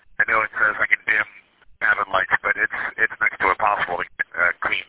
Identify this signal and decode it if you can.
Emergency Alert System (SAME) tones, heard as harsh buzzing bursts, with speech. It is heard from 0.5 to 4.6 s.